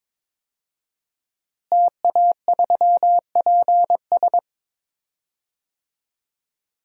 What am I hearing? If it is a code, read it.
Morse code, heard as a keyed tone, about 22 wpm, 700 Hz. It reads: TA3PS
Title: DXCC-100 Call Signs - 5 Characters - Repeated 3x - 22wpm